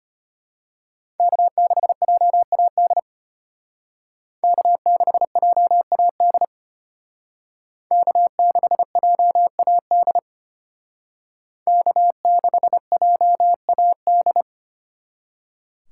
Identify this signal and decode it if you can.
Morse code, heard as a keyed tone, about 32 wpm, 700 Hz. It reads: K6JAD K6JAD K6JAD K6JAD